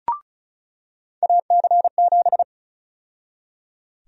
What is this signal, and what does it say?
Morse code, heard as a keyed tone, about 34 wpm, 700 Hz. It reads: AC7
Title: USA Call Sign Prefixes - 35wpm